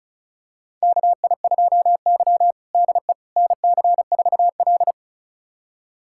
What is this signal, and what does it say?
Morse code, heard as a keyed tone, about 36 wpm, 700 Hz. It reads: KI2Y DE NC4L
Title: USA Callsign DE Callsign - 4 Characters - 35wpm